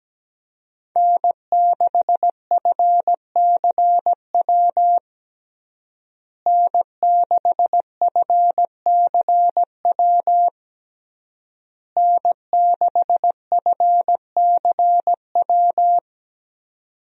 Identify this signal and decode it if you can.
Morse code, heard as a keyed tone, about 17 wpm, 700 Hz. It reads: N6FCW N6FCW N6FCW
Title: USA Call Signs - 5 Characters - Repeated 3x - 17wpm